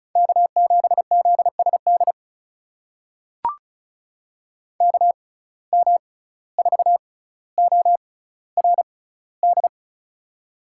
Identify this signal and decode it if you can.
Morse code, heard as a keyed tone, about 36 wpm, spaced as for 17 wpm, 700 Hz. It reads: K7ZSD KM4ORD
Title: USA Call Signs - 6x Character Spacing - 35wpm